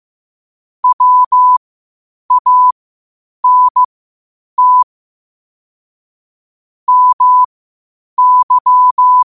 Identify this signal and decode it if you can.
Morse code, heard as a keyed tone, about 15 wpm, spaced as for 8 wpm, 1000 Hz. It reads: WANT MY
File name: qotdfiles / 8wpm